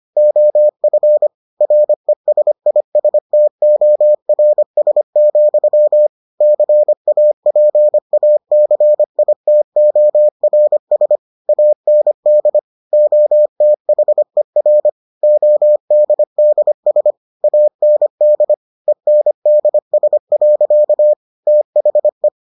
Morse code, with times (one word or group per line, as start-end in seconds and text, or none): 0.2-1.3 OF
1.6-6.1 RESISTORS,
6.4-11.2 CAPACITORS
11.5-12.6 AND
12.9-14.9 OTHER
15.2-17.1 ODDS
17.4-18.5 AND
18.9-21.1 ENDS.
21.5-22.3 THE